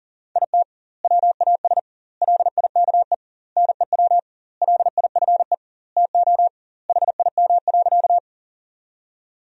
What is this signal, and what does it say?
Morse code, heard as a keyed tone, about 40 wpm, 700 Hz. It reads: IT WAS LIKE NEW LIFE TO HIM.
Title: 2x Word-Spacing - Sentences from Top 300 Words - 40wpm